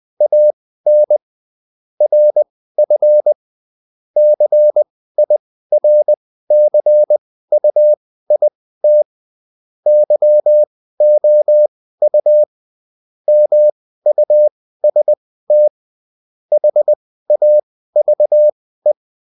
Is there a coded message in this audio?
Yes: Morse code from 0.2 to 18.9 s, heard as a keyed tone, about 20 wpm, spaced as for 14 wpm, 600 Hz. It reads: AN RF CIRCUIT YOU MUST HAVE